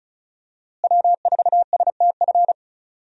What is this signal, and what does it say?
Morse code, heard as a keyed tone, about 34 wpm, 700 Hz. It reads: W4STF
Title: USA Call Signs - 5 Characters - Repeated 2x - 35wpm